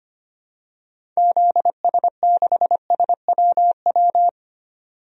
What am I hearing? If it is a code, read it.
Morse code, heard as a keyed tone, about 25 wpm, 700 Hz. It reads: ZS6SWW
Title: DXCC-100 Call Signs - 6 Characters - 25wpm